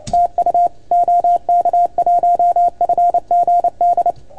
Morse code, heard as a keyed tone, about 30 wpm, 690 Hz, with radio noise. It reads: TU OK1FGD
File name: Archiv / CUC310 / OK1FGD CUC 310 pileup